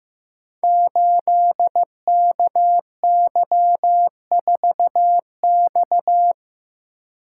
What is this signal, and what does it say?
Morse code, heard as a keyed tone, about 15 wpm, 700 Hz. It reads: 8KY4X
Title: Code Groups - 5 Letter:Numbers - 15wpm